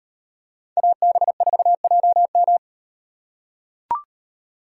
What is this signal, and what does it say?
Morse code, heard as a keyed tone, about 38 wpm, 700 Hz. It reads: AB4JM